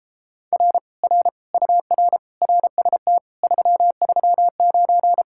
Morse code, heard as a keyed tone, about 29 wpm, 700 Hz. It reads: R R UR RST 339